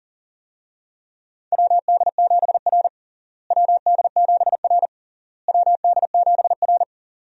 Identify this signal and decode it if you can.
Morse code, heard as a keyed tone, about 40 wpm, 700 Hz. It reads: WD7R WD7R WD7R
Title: USA Call Signs - 4 Characters - Repeated 3x - 40wpm